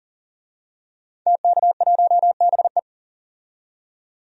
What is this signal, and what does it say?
Morse code, heard as a keyed tone, about 39 wpm, 700 Hz. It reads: TK1BE
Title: DXCC-100 Call Signs - 5 Characters - 40wpm